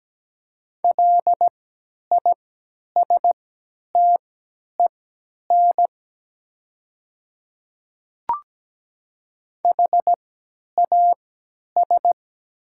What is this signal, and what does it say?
Morse code, heard as a keyed tone, about 17 wpm, spaced as for 10 wpm, 700 Hz. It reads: LISTEN HAS
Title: 3x Character Spacing - Top 500 Words - 17wpm